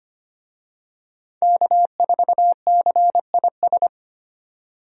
Morse code, heard as a keyed tone, about 25 wpm, 700 Hz. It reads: K4CIS
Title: USA Call Signs - 5 Characters - 25wpm